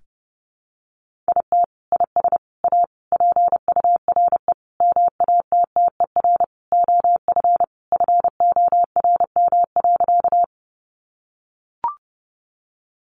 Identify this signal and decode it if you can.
Morse code, heard as a keyed tone, about 30 wpm, 700 Hz. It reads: IT IS A PURE MATTER OF FORM.
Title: Sentences from Top 1500 Words 30wpm